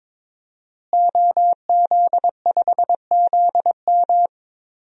Morse code, heard as a keyed tone, about 22 wpm, 700 Hz. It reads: OZ5ZM
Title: DXCC-100 Call Signs - 5 Characters - Repeated 3x - 22wpm